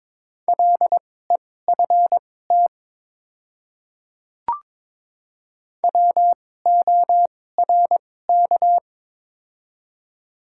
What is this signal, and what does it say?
Morse code, heard as a keyed tone, about 21 wpm, spaced as for 16 wpm, 700 Hz. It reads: LEFT WORK
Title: 2x Character Spacing - Top 300 Words - 22wpm